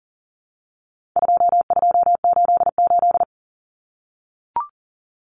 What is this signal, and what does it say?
Morse code, heard as a keyed tone, about 40 wpm, 700 Hz. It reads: 2288
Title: Common Cut Numbers - 4-Digit 40wpm